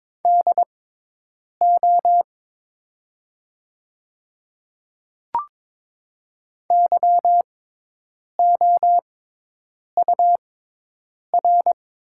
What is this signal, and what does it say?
Morse code, heard as a keyed tone, about 23 wpm, spaced as for 8 wpm, 700 Hz. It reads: DO YOUR
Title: 6x Character Spacing - Top 100 Words - 22wpm